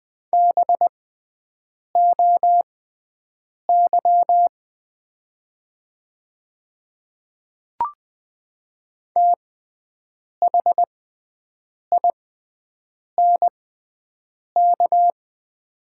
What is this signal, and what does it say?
Morse code, heard as a keyed tone, about 20 wpm, spaced as for 7 wpm, 700 Hz. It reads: BOY THINK
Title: 6x Character Spacing - Top 200 Words - 20wpm